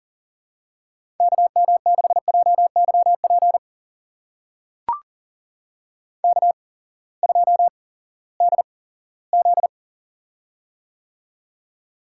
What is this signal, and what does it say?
Morse code, heard as a keyed tone, about 38 wpm, spaced as for 20 wpm, 700 Hz. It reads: KM6JYP K2DZ
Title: USA Call Signs - 8x Character Spacing - 40wpm